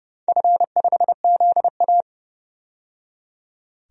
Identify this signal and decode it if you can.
Morse code, heard as a keyed tone, about 30 wpm, 700 Hz. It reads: F5ZA
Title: DXCC-100 Call Signs - 4 Characters - 30wpm